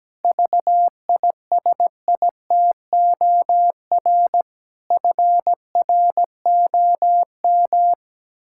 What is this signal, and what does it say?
Morse code, heard as a keyed tone, about 17 wpm, 700 Hz. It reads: VISITOR FROM
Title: Pangram Challenge v2 - 17wpm